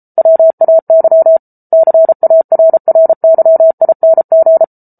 Morse code, heard as a keyed tone, about 34 wpm, 650 Hz. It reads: WAY CARRYING